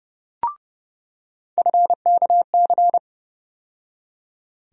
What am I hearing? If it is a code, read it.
Morse code, heard as a keyed tone, about 30 wpm, 700 Hz. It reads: FKC
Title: USA Call Sign Postfixes - 3 Letters 30wpm